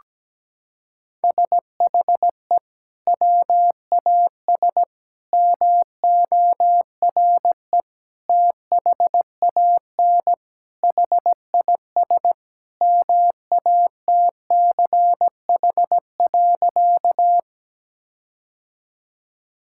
Morse code, heard as a keyed tone, about 17 wpm, 700 Hz. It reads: SHE WAS MORE THAN HIS MATCH.
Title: Sentences from Top 1600 Words 17wpm